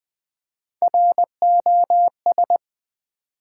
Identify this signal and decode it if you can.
Morse code, heard as a keyed tone, about 20 wpm, 700 Hz. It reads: ROS
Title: Build A Word - 1 Letter At A Time - 4-Letter Words - 20wpm